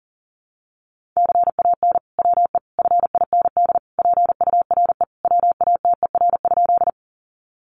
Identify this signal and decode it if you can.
Morse code, heard as a keyed tone, about 37 wpm, 700 Hz. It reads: CAN WE FIND PURE WATER?
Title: Sentences from Top 1500 Words 40wpm